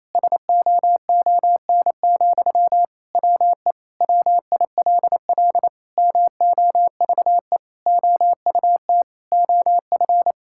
Morse code, heard as a keyed tone, about 27 wpm, 700 Hz. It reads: SOON, WE WILL MOVE OUT OF
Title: Sentences from Top 1000 Words - Review Part 6 - 28wpm